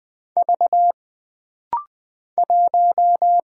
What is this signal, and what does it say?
Morse code, heard as a keyed tone, about 20 wpm, 700 Hz. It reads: V 1
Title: Single Letter-Number - Rapid-Fire 20wpm